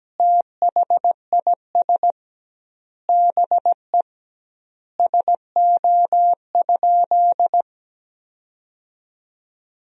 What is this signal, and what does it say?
Morse code, heard as a keyed tone, about 17 wpm, 700 Hz. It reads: THIS BE SO?
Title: 2x Word-Spacing - Sentences from Top 300 Words - 17wpm